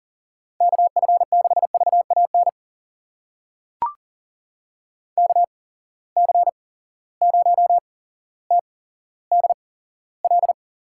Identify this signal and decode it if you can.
Morse code, heard as a keyed tone, about 40 wpm, spaced as for 17 wpm, 700 Hz. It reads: KF6VAN KC0TDL
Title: USA Call Signs - 6 Characters - 8x Character Spacing - 40wpm